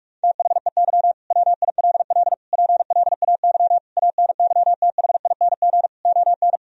Morse code, heard as a keyed tone, about 41 wpm, 700 Hz. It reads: THEY WILL PLAY ANYTHING ON
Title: Sentences from Top 1000 Words - Review Part 4 - 45wpm